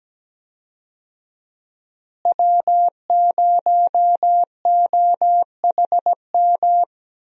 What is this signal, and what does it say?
Morse code, heard as a keyed tone, about 17 wpm, 700 Hz. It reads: W0OHM
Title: USA Call Signs - 5 Characters - Repeated 2x - 17wpm